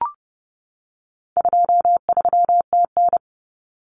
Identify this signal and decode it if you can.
Morse code, heard as a keyed tone, about 30 wpm, 700 Hz. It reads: 23TN
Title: Common Cut Numbers - 4-Digit 30wpm